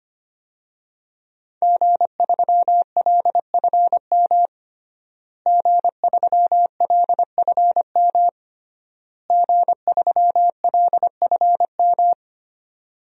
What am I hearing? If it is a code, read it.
Morse code, heard as a keyed tone, about 25 wpm, 700 Hz. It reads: G3LFM G3LFM G3LFM
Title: DXCC-100 Call Signs - 5 Characters - Repeated 3x - 25wpm